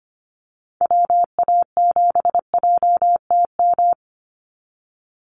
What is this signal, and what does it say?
Morse code, heard as a keyed tone, about 25 wpm, 700 Hz. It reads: WA7JTM
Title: NA SOTA Activator Callsigns 25wpm